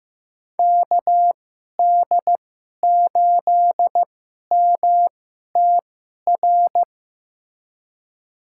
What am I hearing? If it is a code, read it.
Morse code, heard as a keyed tone, about 15 wpm, spaced as for 11 wpm, 700 Hz. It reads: KD8MTR
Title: USA Call Signs - 6 Characters - 2x Character Spacing - 15wpm